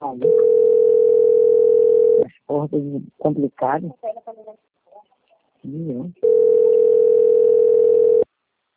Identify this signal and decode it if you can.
Ringback tone (North America), 440 + 480 Hz, with speech.
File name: rg-1600-unknown- - - .18811